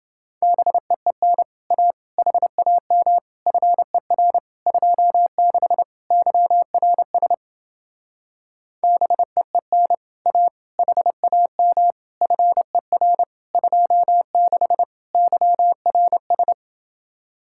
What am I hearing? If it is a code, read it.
Morse code, heard as a keyed tone, about 30 wpm, 700 Hz. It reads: BEEN A HAM FER 26 YRS BEEN A HAM FER 26 YRS